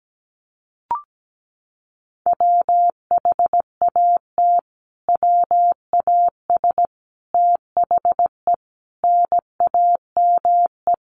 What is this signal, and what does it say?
Morse code, heard as a keyed tone, about 17 wpm, 700 Hz. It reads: WHAT WAS THE NAME
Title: Sentences from Top 300 Words 17wpm